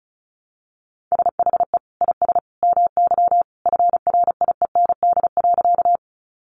Morse code, heard as a keyed tone, about 34 wpm, 700 Hz. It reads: SHE IS MY FRIEND.
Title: Sentences from Top 300 Words 35wpm